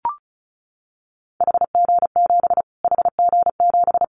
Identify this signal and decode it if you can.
Morse code, heard as a keyed tone, about 34 wpm, 700 Hz. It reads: HG7 HG7
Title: Callsign Prefixes for Top 25 DXCC - Repeated 3x - 35wpm